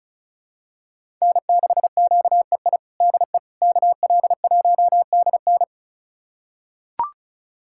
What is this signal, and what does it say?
Morse code, heard as a keyed tone, about 36 wpm, 700 Hz. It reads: N6QEI DE KL1DN
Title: USA Callsign DE Callsign - 5 Characters - 35wpm